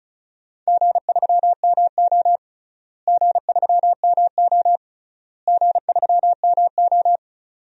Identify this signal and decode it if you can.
Morse code, heard as a keyed tone, about 36 wpm, 700 Hz. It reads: G3MO G3MO G3MO
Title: DXCC-100 Call Signs - 4 Characters - Repeated 3x - 35wpm